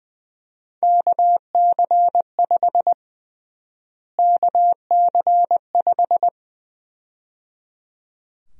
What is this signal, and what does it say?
Morse code, heard as a keyed tone, about 20 wpm, 700 Hz. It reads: KC5 KC5
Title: USA Call Sign Prefixes - Repeated 2x - 20wpm